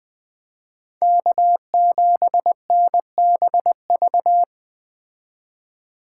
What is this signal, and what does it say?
Morse code, heard as a keyed tone, about 20 wpm, 700 Hz. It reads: K7NBV
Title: Speed Racing - USA 1x3 Call Signs 17wpm